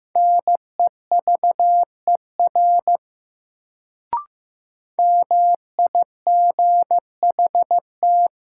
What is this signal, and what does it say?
Morse code, heard as a keyed tone, about 15 wpm, 700 Hz. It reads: NEVER MIGHT
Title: Sentences from Top 300 Words - New Word Drill - 15wpm